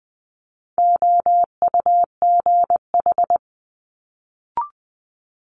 Morse code, heard as a keyed tone, about 20 wpm, 700 Hz. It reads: OUGH
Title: Multi-letter Phonemes Quick Fire 20wpm